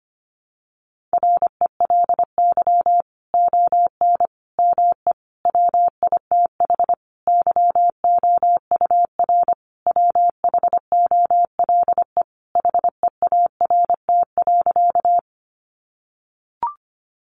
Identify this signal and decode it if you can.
Morse code, heard as a keyed tone, about 25 wpm, 700 Hz. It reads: RELY ON ME WITH YOUR WHOLE HEART.
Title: Sentences from Top 1200 Words 25wpm